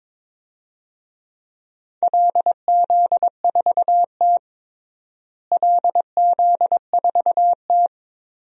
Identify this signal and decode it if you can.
Morse code, heard as a keyed tone, about 21 wpm, 700 Hz. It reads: LZ4T LZ4T
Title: DXCC-100 Call Signs - 4 Characters - Repeated 2x - 22wpm